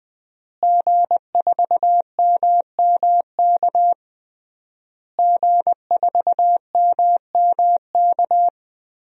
Morse code, heard as a keyed tone, about 20 wpm, 700 Hz. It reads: G4MMK G4MMK
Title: DXCC-100 Call Signs - 5 Characters - Repeated 3x - 20wpm